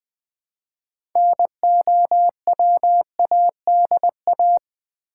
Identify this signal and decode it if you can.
Morse code, heard as a keyed tone, about 20 wpm, 700 Hz. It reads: NOWADA
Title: Build A Word - 2 Letters At A Time - 8-Letter Words - 20wpm